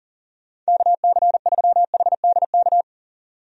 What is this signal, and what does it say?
Morse code, heard as a keyed tone, about 37 wpm, 700 Hz. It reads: KC3HDK
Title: USA Call Signs - 6 Characters - 40wpm